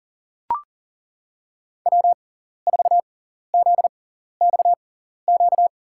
Morse code, heard as a keyed tone, about 37 wpm, spaced as for 14 wpm, 700 Hz. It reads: W4ZXQ
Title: USA Call Signs - 5 Characters - 6x Character Spacing - 40wpm